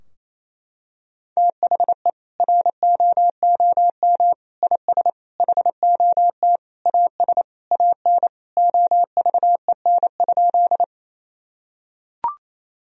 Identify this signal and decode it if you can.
Morse code, heard as a keyed tone, about 28 wpm, 700 Hz. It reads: THE ROOM IS HOT AS AN OVEN?
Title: Sentences from Top 1000 Words - Review Part 4 - 28wpm